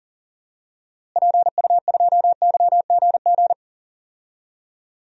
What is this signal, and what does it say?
Morse code, heard as a keyed tone, about 40 wpm, 700 Hz. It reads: PU2YGG
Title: DXCC-100 Call Signs - 6 Characters - Repeated 3x - 40wpm